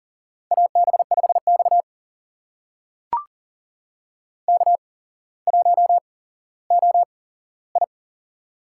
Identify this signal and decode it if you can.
Morse code, heard as a keyed tone, about 38 wpm, spaced as for 17 wpm, 700 Hz. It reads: AB5X K1OI
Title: USA Call Signs - 4 Characters - 8x Character Spacing - 40wpm